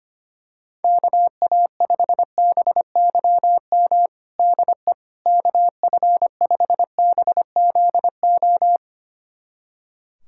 Morse code, heard as a keyed tone, about 25 wpm, 700 Hz. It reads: KA5BYM DE KF5BZO
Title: USA Callsign DE Callsign 25wpm